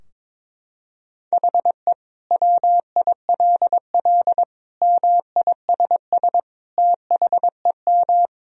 Morse code, heard as a keyed tone, about 22 wpm, 700 Hz. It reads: HE WILL MISS THEM
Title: Sentences from Top 1000 Words - Review Part 3 - 22wpm